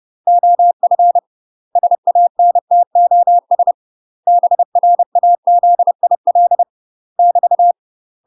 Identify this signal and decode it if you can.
Morse code, heard as a keyed tone, about 30 wpm, 700 Hz. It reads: OF SANTOS BRAZIL <BT>
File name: UK MASTER QSOs TRAINING/30WPM